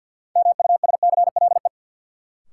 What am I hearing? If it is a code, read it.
Morse code, heard as a keyed tone, about 53 wpm, 700 Hz. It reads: MUSCLE
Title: Sentences from Top 1300 Words - New Word Drill Alt - 50wpm